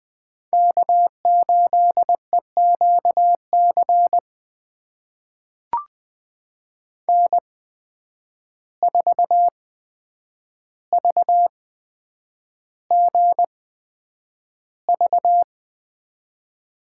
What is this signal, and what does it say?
Morse code, heard as a keyed tone, about 20 wpm, spaced as for 9 wpm, 700 Hz. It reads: K8EQC N4VGV
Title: USA Call Signs - 5 Characters - 8x Character Spacing - 20wpm